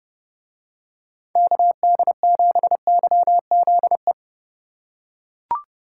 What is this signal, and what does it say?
Morse code, heard as a keyed tone, about 29 wpm, 700 Hz. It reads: KD7YZE